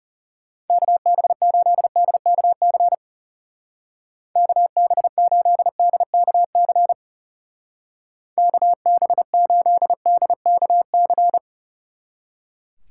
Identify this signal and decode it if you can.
Morse code, heard as a keyed tone, about 36 wpm, 700 Hz. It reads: KB8DKC KB8DKC KB8DKC
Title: Speed Racing - USA 2x3 Call Signs 30wpm